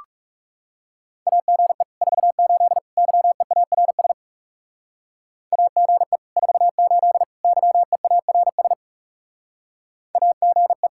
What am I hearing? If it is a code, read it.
Morse code, heard as a keyed tone, about 36 wpm, 700 Hz. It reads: AGE 48 YEARS AGE 48 YEARS AGE
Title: Speed-Racing Common QSO Phrases - 30wpm